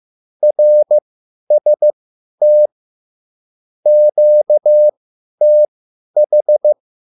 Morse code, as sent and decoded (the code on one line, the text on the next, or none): .-. ... - / --.- - ....
RST QTH